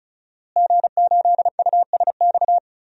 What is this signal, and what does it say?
Morse code, heard as a keyed tone, about 35 wpm, 700 Hz. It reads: G8USX
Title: DXCC-100 Call Signs - 5 Characters - Repeated 3x - 35wpm